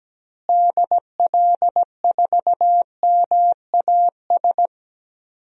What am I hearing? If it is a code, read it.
Morse code, heard as a keyed tone, about 17 wpm, 700 Hz. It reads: DL4MAS
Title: DXCC-100 Call Signs - 6 Characters - Repeated 3x - 17wpm